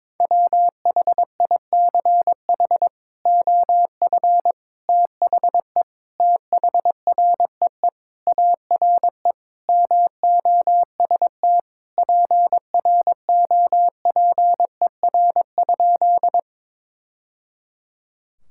Morse code, heard as a keyed tone, about 21 wpm, 700 Hz. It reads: WHICH OF THE THREE ARE MOST PROPER?
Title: Sentences from Top 1000 Words - Review Part 5 - 22wpm